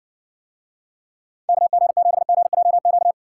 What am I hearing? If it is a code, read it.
Morse code, heard as a keyed tone, about 60 wpm, 700 Hz. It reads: KG7GJQ